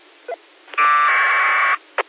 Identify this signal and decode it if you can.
Packet radio (AX.25, as in APRS).
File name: qsorec SimplexLogic